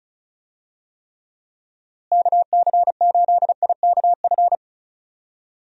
Morse code, heard as a keyed tone, about 36 wpm, 700 Hz. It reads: KC8IKF
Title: USA Call Signs - 6 Characters - Repeated 3x - 35wpm